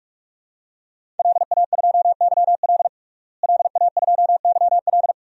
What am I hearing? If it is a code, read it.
Morse code, heard as a keyed tone, about 42 wpm, 700 Hz. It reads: LA2YL LA2YL
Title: DXCC-100 Call Signs - 5 Characters - Repeated 2x - 45wpm